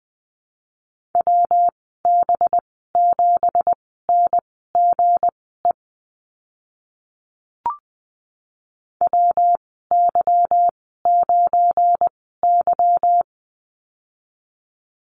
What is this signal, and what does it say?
Morse code, heard as a keyed tone, about 20 wpm, spaced as for 15 wpm, 700 Hz. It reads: WB7NGE WY9Y
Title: 186 Lesson 30g - Letter b - Call Signs 20wpm